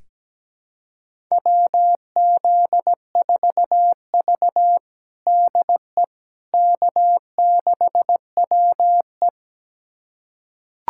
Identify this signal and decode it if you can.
Morse code, heard as a keyed tone, about 17 wpm, 700 Hz. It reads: WZ4V DE K6WE
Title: USA Callsign DE Callsign - 4 Characters - 17wpm